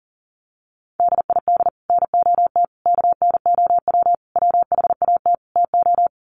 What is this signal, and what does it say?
Morse code, heard as a keyed tone, about 38 wpm, 700 Hz. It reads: DID NOT KNOW WHAT TO
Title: Sets of 5 Words 40wpm